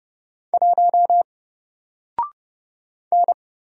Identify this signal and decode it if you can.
Morse code, heard as a keyed tone, about 30 wpm, 700 Hz. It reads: 1 N
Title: Single Letter-Number 30wpm